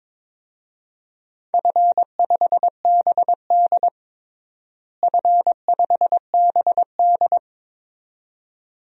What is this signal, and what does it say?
Morse code, heard as a keyed tone, about 22 wpm, 700 Hz. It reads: F5BD F5BD
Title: DXCC-100 Call Signs - 4 Characters - Repeated 2x - 22wpm